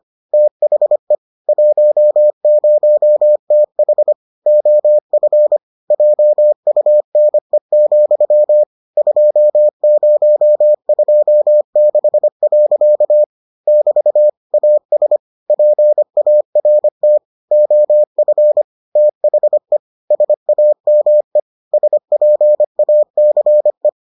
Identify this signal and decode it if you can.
Morse code, heard as a keyed tone, about 25 wpm, 600 Hz. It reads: THE 10TH OF JUNE, 2026. <BT> AS PART OF THE SAME SPACE